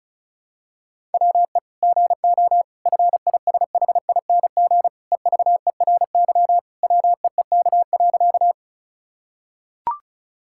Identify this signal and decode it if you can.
Morse code, heard as a keyed tone, about 36 wpm, 700 Hz. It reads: WE GO FISHING EVERY WEEK.
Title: Sentences from Top 1000 Words - Review Part 5 - 35wpm